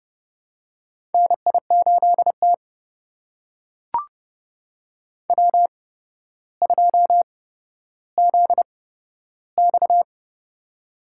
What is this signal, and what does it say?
Morse code, heard as a keyed tone, about 30 wpm, spaced as for 13 wpm, 700 Hz. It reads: NI8T W2ZX
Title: USA Call Signs - 4 Characters - 8x Character Spacing - 30wpm